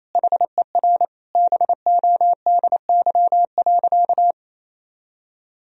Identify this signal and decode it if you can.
Morse code, heard as a keyed tone, about 27 wpm, 700 Hz. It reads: HER BODY.
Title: Sentences from Top 1000 Words - Review Part 2 - 28wpm